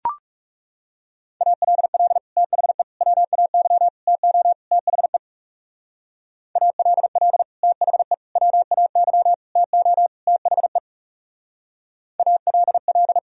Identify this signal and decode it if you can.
Morse code, heard as a keyed tone, about 35 wpm, 700 Hz. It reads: ALL THE WAY TO THE ALL THE WAY TO THE ALL
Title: Speed-Racing - Sets of 5 Words 30wpm